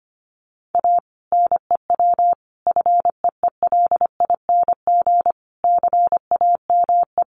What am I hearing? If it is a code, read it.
Morse code, heard as a keyed tone, about 25 wpm, 700 Hz. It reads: A NEW FEELING CAME
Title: Sentences from Top 1200 Words 25wpm